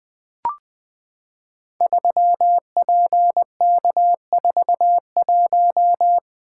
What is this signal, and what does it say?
Morse code, heard as a keyed tone, about 20 wpm, 700 Hz. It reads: 3PK41
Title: Code Groups - 5 Letter:Numbers - 20wpm